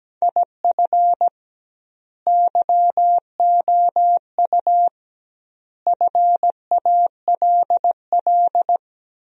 Morse code, heard as a keyed tone, about 17 wpm, 700 Hz. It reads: IF YOU FALL
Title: 2x Word-Spacing - Sentences from Top 500 Words - 17wpm